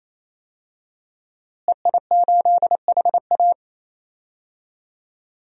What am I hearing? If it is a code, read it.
Morse code, heard as a keyed tone, about 28 wpm, 700 Hz. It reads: EI8HA
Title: DXCC-100 Call Signs - 5 Characters - Repeated 3x - 28wpm